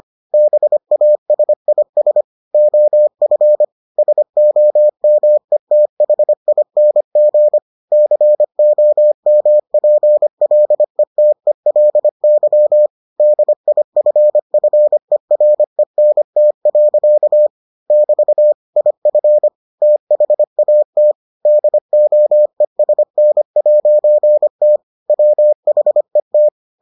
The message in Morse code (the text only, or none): BASIS OF SOMETHING COMPLETELY DIFFERENT. <BT> IF THAT DOESN'T WHET